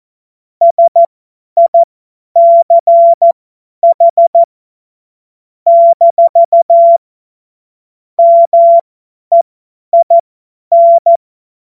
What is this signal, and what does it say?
Morse code, heard as a keyed tone, about 14 wpm, spaced as for 10 wpm, 680 Hz. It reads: SICH - MEIN
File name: Morsekurs / Witze / Witz 06. Dezember 2022 50 F70